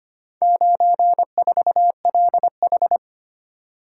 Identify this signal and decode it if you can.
Morse code, heard as a keyed tone, about 25 wpm, 700 Hz. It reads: 94LH
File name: practice / 25wpm / Four Letter-Number Combinations - 25wpm